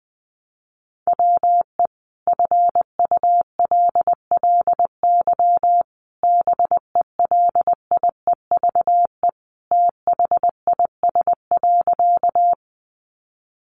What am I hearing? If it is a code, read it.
Morse code, heard as a keyed tone, about 20 wpm, 700 Hz. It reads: WE FULLY BELIEVE THIS.
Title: Sentences from Top 1300 Words 20wpm